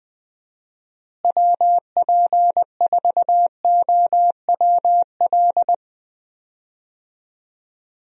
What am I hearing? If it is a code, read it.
Morse code, heard as a keyed tone, about 20 wpm, 700 Hz. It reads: WP4OWL
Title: USA Call Signs - 6 Characters - Repeated 2x - 20wpm